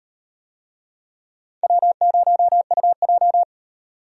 Morse code, heard as a keyed tone, about 38 wpm, 700 Hz. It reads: W0UJ